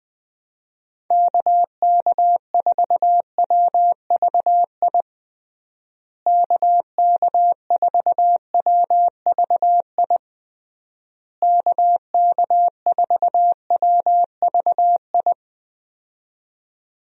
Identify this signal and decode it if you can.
Morse code, heard as a keyed tone, about 19 wpm, 700 Hz. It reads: KK4WVI KK4WVI KK4WVI
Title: USA Call Signs - 6 Characters - Repeated 3x - 20wpm